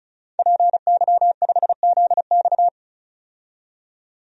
Morse code, heard as a keyed tone, about 36 wpm, 700 Hz. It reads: PY5ZX
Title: DXCC-100 Call Signs - 5 Characters - 35wpm